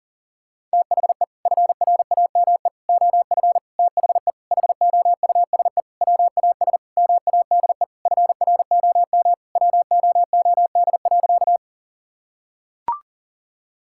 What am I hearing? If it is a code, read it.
Morse code, heard as a keyed tone, about 37 wpm, 700 Hz. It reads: THE FRAME OF THE HOUSE WAS MADE FROM WOOD.
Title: Sentences from Top 1000 Words - Review Part 6 - 40wpm